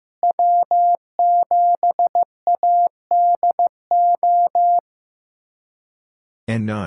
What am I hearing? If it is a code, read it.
Morse code, heard as a keyed tone, about 15 wpm, 700 Hz. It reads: W7ADO
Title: USA Callsign DE Callsign - 5 Characters - 15wpm